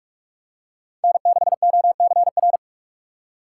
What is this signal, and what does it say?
Morse code, heard as a keyed tone, about 45 wpm, 700 Hz. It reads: N6OCR